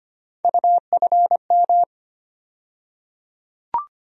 Morse code, heard as a keyed tone, about 25 wpm, 700 Hz. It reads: UFM
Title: USA Call Sign Postfixes - 3 Letters 25wpm